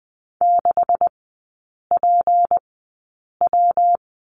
Morse code, heard as a keyed tone, about 20 wpm, spaced as for 8 wpm, 700 Hz. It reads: 6PW
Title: 212 Lesson 34e - Number 8 - Three Characters 20wpm